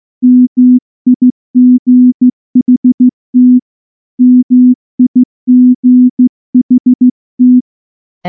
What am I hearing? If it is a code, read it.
Morse code, heard as a keyed tone, about 14 wpm, 250 Hz. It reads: MIGHT MIGHT
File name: morse-training-15WPM